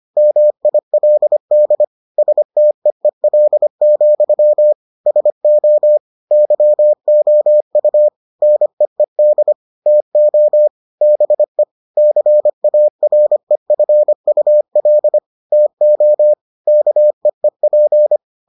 Morse code, heard as a keyed tone, about 25 wpm, 600 Hz. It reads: MILD STEEL, SO YOU NEED TO BE CAREFUL TO KEEP